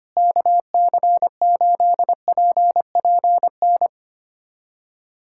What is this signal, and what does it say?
Morse code, heard as a keyed tone, about 25 wpm, 700 Hz. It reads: KC8PPN
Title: USA Callsign DE Callsign - 6 Characters - 25wpm